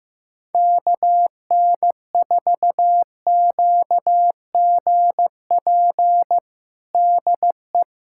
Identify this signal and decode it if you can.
Morse code, heard as a keyed tone, about 15 wpm, 700 Hz. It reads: KN4QGP DE
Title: USA Callsign DE Callsign 15wpm